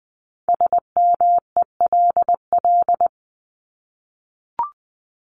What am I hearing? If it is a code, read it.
Morse code, heard as a keyed tone, about 20 wpm, 700 Hz. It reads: SMELL
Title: Top 1501-1750 Words 20wpm